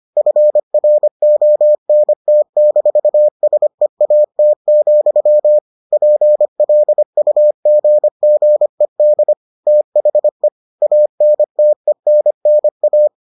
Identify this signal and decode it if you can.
Morse code, heard as a keyed tone, about 25 wpm, 600 Hz. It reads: FRONT-SEAT, PLUGGED THE ANTENNA